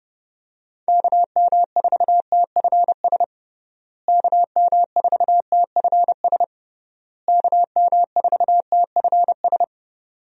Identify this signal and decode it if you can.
Morse code, heard as a keyed tone, about 30 wpm, 700 Hz. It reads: KM4TFS KM4TFS KM4TFS
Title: USA Call Signs - 6 Characters - Repeated 3x - 30wpm